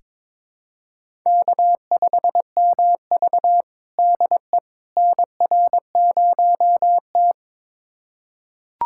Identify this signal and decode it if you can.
Morse code, heard as a keyed tone, about 21 wpm, 700 Hz. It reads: K5MV DE NR0T
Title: USA Callsign DE Callsign - 4 Characters - 22wpm